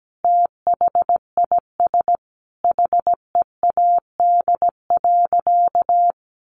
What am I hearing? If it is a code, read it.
Morse code, heard as a keyed tone, about 17 wpm, 700 Hz. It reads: THIS HEAD.
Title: Sentences from Top 1500 Words 17wpm